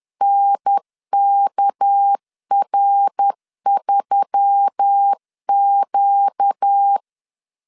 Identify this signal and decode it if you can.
Morse code, heard as a keyed tone, about 10 wpm, 800 Hz. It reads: NKR3Q
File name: CW / DET / 1 9 1500C S A 10wpm 4511